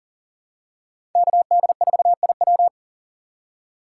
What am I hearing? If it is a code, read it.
Morse code, heard as a keyed tone, about 40 wpm, 700 Hz. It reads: KD4IW
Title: USA Call Signs - 5 Characters - 2x Character Spacing - 40wpm